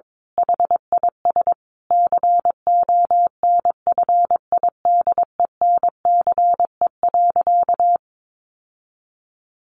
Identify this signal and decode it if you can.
Morse code, heard as a keyed tone, about 22 wpm, 700 Hz. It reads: HIS CONFIDENCE.
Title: Sentences from Top 1500 Words 22wpm